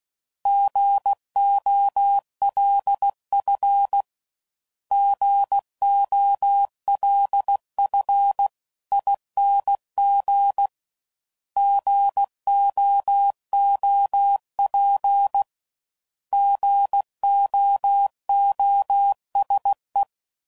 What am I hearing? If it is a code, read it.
Morse code, heard as a keyed tone, about 16 wpm, 800 Hz. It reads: GOLF GOLFING GOOP GOOSE
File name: Prefixes / By First Letter / Gs